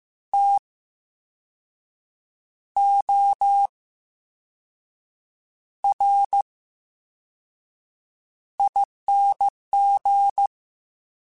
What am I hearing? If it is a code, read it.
Morse code, heard as a keyed tone, about 15 wpm, 800 Hz. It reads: T O R I N G
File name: Special INGs-mp3s / Special INGs-mp3s / VerySlowToFaster Ing @13